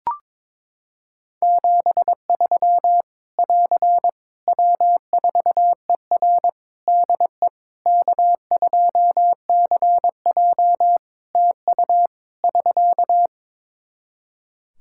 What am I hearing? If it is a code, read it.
Morse code, heard as a keyed tone, about 22 wpm, 700 Hz. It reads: 73 <AR> W4ER DE K2CJ TU <SK>
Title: Practice QSO - Exchange 4 - Ending - 22wpm